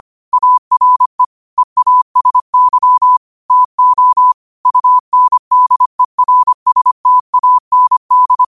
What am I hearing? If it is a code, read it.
Morse code, heard as a keyed tone, about 25 wpm, 1000 Hz. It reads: ARE EASY TO UNDERSTAND
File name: qotdfiles / 25wpm